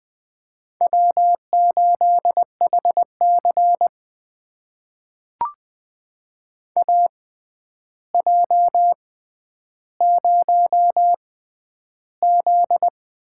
Morse code, heard as a keyed tone, about 21 wpm, spaced as for 10 wpm, 700 Hz. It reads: W8HC AJ0Z
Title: USA Call Signs - 4 Characters - 6x Character Spacing - 20wpm